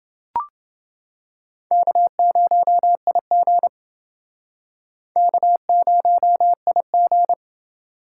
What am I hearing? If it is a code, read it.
Morse code, heard as a keyed tone, about 29 wpm, 700 Hz. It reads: K0IG K0IG